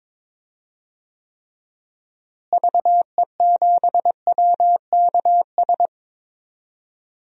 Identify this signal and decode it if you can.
Morse code, heard as a keyed tone, about 22 wpm, 700 Hz. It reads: VE7WKS
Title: DXCC-100 Call Signs - 6 Characters - Repeated 3x - 22wpm